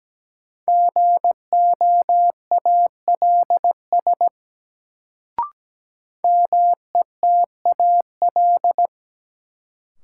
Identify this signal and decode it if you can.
Morse code, heard as a keyed tone, about 17 wpm, 700 Hz. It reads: GOALS METAL
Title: Sentences from Top 1000 Words - Review Part 6 - New Word Drill - 17wpm